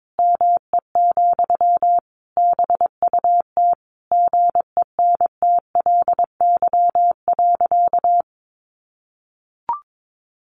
Morse code, heard as a keyed tone, about 22 wpm, 700 Hz. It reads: ME, BUT GENTLY.
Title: Sentences from Top 1500 Words 22wpm